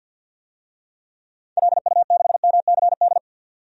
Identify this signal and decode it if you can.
Morse code, heard as a keyed tone, about 52 wpm, 700 Hz. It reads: LU6MCD
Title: DXCC-100 Call Signs - 6 Characters - 50wpm